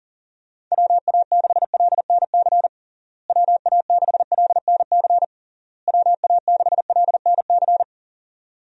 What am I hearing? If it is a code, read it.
Morse code, heard as a keyed tone, about 37 wpm, 700 Hz. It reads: WA6LNC WA6LNC WA6LNC
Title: USA Call Signs - 6 Characters - Repeated 3x - 40wpm